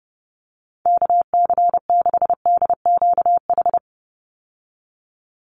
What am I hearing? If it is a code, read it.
Morse code, heard as a keyed tone, about 30 wpm, 700 Hz. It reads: KC6DQH
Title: USA Call Signs 30wpm